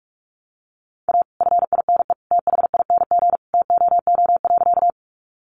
Morse code, heard as a keyed tone, about 44 wpm, 700 Hz. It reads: A FINE THING TOO.
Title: Sentences from Top 500 Words 45wpm